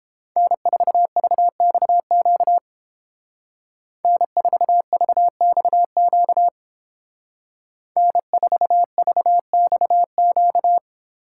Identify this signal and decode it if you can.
Morse code, heard as a keyed tone, about 30 wpm, 700 Hz. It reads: N4VXQ N4VXQ N4VXQ